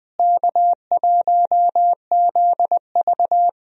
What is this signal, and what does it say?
Morse code, heard as a keyed tone, about 20 wpm, 700 Hz. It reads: K1ZV
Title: USA Call Signs - 4 Characters - Repeated 2x - 20wpm